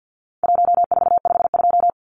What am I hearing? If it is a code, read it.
Morse code, heard as a keyed tone, about 50 wpm, 700 Hz. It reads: 1452
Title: Numbers - Contest Style - 4-Digit 50wpm